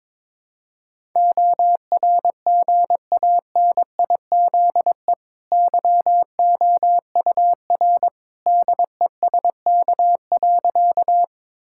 Morse code, heard as a keyed tone, about 22 wpm, 690 Hz. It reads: ORGANIZE YOUR DESK.
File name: practice / 22wpm / Easy Sentences with Hints - 22wpm